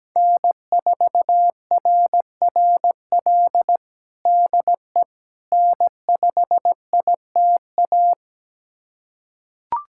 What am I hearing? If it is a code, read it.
Morse code, heard as a keyed tone, about 17 wpm, 700 Hz. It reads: N4RRL DE N5ITA
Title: USA Callsign DE Callsign - 5 Characters - 17wpm